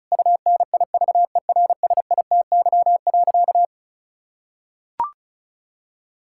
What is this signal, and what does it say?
Morse code, heard as a keyed tone, about 36 wpm, 700 Hz. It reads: UNIVERSITY.
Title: Sentences from Top 1000 Words - Review Part 5 - 35wpm